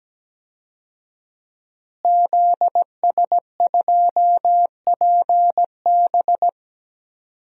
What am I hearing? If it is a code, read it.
Morse code, heard as a keyed tone, about 17 wpm, 700 Hz. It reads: ZS2PB
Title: DXCC-100 Call Signs - 5 Characters - Repeated 2x - 17wpm